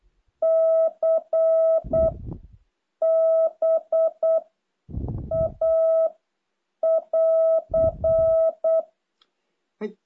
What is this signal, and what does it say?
Morse code, heard as a keyed tone, about 8 wpm, 660 Hz. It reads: C B A <AR>
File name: jh8cbh / 2023cw